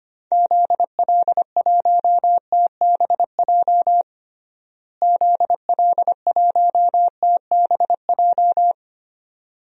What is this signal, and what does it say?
Morse code, heard as a keyed tone, about 25 wpm, 700 Hz. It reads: ZL1TBJ ZL1TBJ
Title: DXCC-100 Call Signs - 6 Characters - Repeated 2x - 25wpm